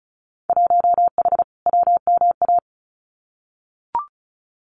Morse code, heard as a keyed tone, about 34 wpm, 700 Hz. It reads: 1H WMA
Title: Winter Field Day Exchanges 35wpm